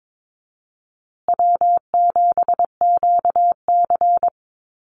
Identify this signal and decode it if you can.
Morse code, heard as a keyed tone, about 22 wpm, 690 Hz. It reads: W7QC
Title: Call Signs Repeated 2x - 22wpm